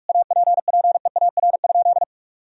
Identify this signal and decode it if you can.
Morse code, heard as a keyed tone, about 46 wpm, 700 Hz. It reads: APPEAR?
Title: Sentences from Top 1000 Words - Review Part 2 - 45wpm